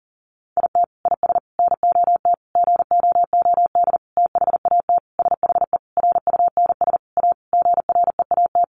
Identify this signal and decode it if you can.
Morse code, heard as a keyed tone, about 40 wpm, 700 Hz. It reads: IT IS NOT GOOD THAT SHE RUNS A GREAT
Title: Sentences from Top 600 Words 40wpm